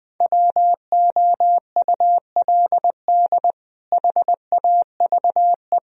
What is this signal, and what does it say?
Morse code, heard as a keyed tone, about 21 wpm, 700 Hz. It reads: WOULD HAVE
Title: Speed-Racing - Sets of 3 Words 17wpm